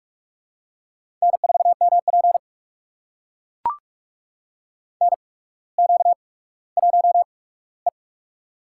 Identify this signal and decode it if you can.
Morse code, heard as a keyed tone, about 44 wpm, spaced as for 19 wpm, 700 Hz. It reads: N4MP NQ1E
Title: USA Call Signs - 4 Characters - 8x Character Spacing - 45wpm